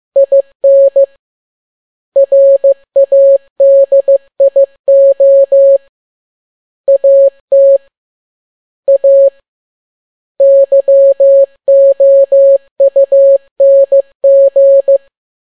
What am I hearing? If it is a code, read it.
Morse code, heard as a keyed tone, about 15 wpm, 550 Hz. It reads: IN RADIO AT A YOUNG
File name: n6tt p1 15C13